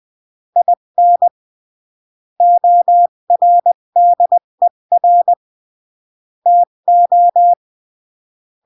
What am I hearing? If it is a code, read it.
Morse code, heard as a keyed tone, about 20 wpm, 700 Hz. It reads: IN ORDER TO
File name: CW / jodorowsky / 82 commandments 20-15